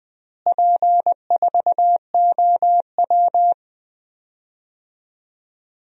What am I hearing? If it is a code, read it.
Morse code, heard as a keyed tone, about 20 wpm, 700 Hz. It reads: P4OW
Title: DXCC-100 Call Signs - 4 Characters - Repeated 2x - 20wpm